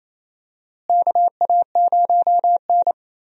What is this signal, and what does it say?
Morse code, heard as a keyed tone, about 28 wpm, 700 Hz. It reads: KA0N
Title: USA Call Signs - 4 Characters - Repeated 3x - 28wpm